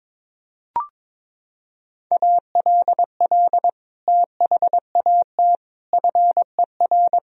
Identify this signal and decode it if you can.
Morse code, heard as a keyed tone, about 22 wpm, 700 Hz. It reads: ALL THAT FER
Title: Sentences from Top 500 Words and Abbreviations 22wpm